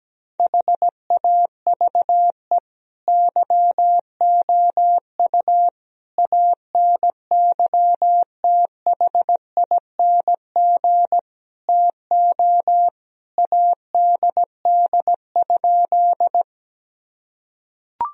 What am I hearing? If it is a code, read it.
Morse code, heard as a keyed tone, about 17 wpm, 700 Hz. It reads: HAVE YOU ANYTHING TO ADD?
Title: Sentences from Top 1000 Words - Review Part 1 - 17wpm